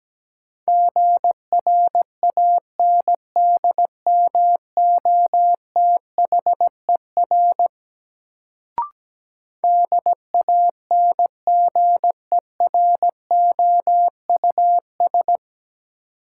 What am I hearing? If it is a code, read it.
Morse code, heard as a keyed tone, about 17 wpm, 700 Hz. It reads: GRANDMOTHER DANGEROUS
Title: Sentences from Top 1200 Words - New Word Drill Alt - 17wpm